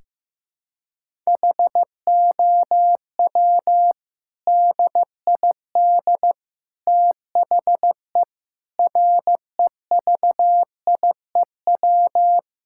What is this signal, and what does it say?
Morse code, heard as a keyed tone, about 15 wpm, 700 Hz. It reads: HOW DID THE REVIEW
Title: Sentences from Top 1000 Words - Review Part 6 - 15wpm